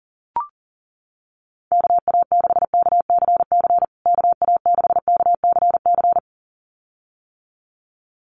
Morse code, heard as a keyed tone, about 40 wpm, 700 Hz. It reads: KA6KCC KA6KCC
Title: USA Call Signs Repeated 2x 40wpm